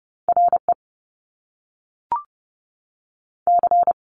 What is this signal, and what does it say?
Morse code, heard as a keyed tone, about 30 wpm, 700 Hz. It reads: R E C
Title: Top Two Letter Combinations in Words 30wpm